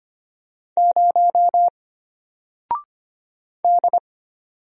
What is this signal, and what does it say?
Morse code, heard as a keyed tone, about 25 wpm, 700 Hz. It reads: 0 D
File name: Single-Letter-Number-25wpm